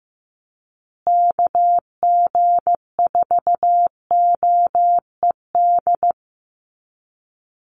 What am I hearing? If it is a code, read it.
Morse code, heard as a keyed tone, about 15 wpm, 700 Hz. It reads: KG4OED
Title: USA Call Signs Repeated 3x 15wpm